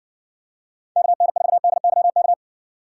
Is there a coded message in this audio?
Yes: Morse code from 1.0 to 2.3 s, heard as a keyed tone, about 62 wpm, 700 Hz. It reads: KN4DYK